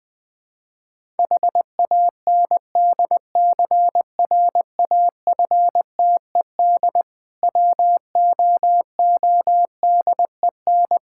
Morse code, heard as a keyed tone, about 20 wpm, 690 Hz. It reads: HANDCRAFTED WOODEN
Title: Difficult Words with Hints - 20wpm